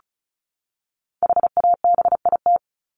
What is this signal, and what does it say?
Morse code, heard as a keyed tone, about 34 wpm, 700 Hz. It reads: HABIT
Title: Top 1751-2000 Words 35wpm